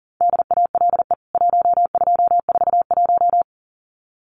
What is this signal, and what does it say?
Morse code, heard as a keyed tone, about 40 wpm, 700 Hz. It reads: DALE 1241
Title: CWT Exchange 40wpm